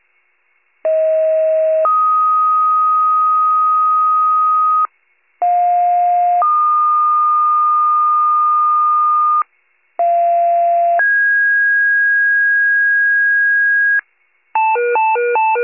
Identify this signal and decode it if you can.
Two-tone page, heard as a one-second tone, then a three-second tone, about 640 then 1250 Hz, about 710 then 1220 Hz, about 690 then 1650 Hz. It is heard from 0.8 to 14.0 s.
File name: pager / 46 RESCUE / wcalert